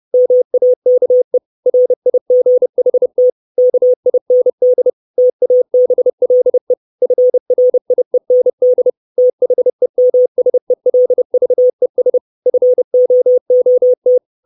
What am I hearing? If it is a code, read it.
Morse code, heard as a keyed tone, about 30 wpm, 500 Hz. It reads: MAKE RIGHT KIND TABLE FRIEND THEMSELVES FOOT
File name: 632 Words 30WPM